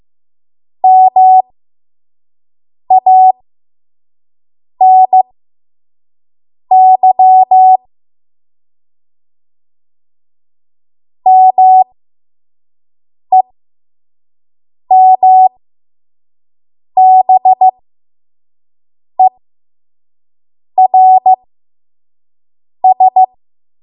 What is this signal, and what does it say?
Morse code, heard as a keyed tone, about 15 wpm, spaced as for 5 wpm, 750 Hz. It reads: MANY MEMBERS
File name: Morse / 05WPM